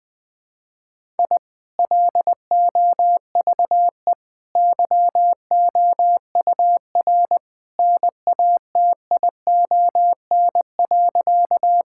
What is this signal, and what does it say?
Morse code, heard as a keyed tone, about 20 wpm, 700 Hz. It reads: I LOVE YOUR NATION.
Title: Sentences from Top 1000 Words - Review Part 6 - 20wpm